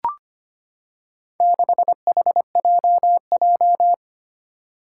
Morse code, heard as a keyed tone, about 25 wpm, 700 Hz. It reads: 6HJJ
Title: Four Challenging Letter-Number Combinations 25wpm